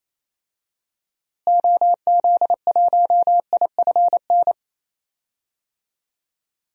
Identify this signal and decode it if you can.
Morse code, heard as a keyed tone, about 27 wpm, 700 Hz. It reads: OZ1IFN
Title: DXCC-100 Call Signs - 6 Characters - Repeated 3x - 28wpm